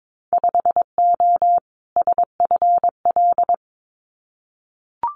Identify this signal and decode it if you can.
Morse code, heard as a keyed tone, about 22 wpm, 700 Hz. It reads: 5O SFL
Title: Winter Field Day Exchanges 22wpm